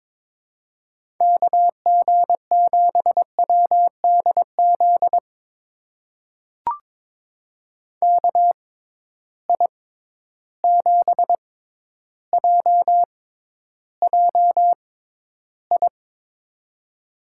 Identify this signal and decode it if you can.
Morse code, heard as a keyed tone, about 22 wpm, spaced as for 11 wpm, 700 Hz. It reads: KG7WDZ KI7JJI
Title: USA Call Signs - 6 Characters - 6x Character Spacing - 22wpm